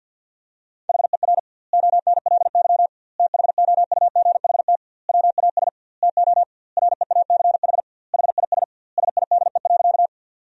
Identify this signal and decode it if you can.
Morse code, heard as a keyed tone, about 50 wpm, 700 Hz. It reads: HER ONLY THOUGHT WAS TO REACH HIS SIDE.
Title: 2x Word-Spacing - Sentences from Top 400 Words - 50wpm